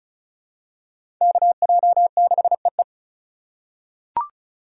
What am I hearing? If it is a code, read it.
Morse code, heard as a keyed tone, about 36 wpm, 700 Hz. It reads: KJ6EE
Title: USA Call Signs - 5 Characters - 4x Character Spacing - 35wpm